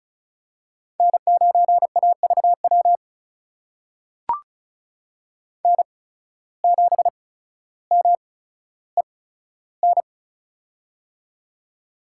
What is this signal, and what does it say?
Morse code, heard as a keyed tone, about 36 wpm, spaced as for 15 wpm, 700 Hz. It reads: N9AVW N7MEN
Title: USA Call Signs - 5 Characters - 8x Character Spacing - 35wpm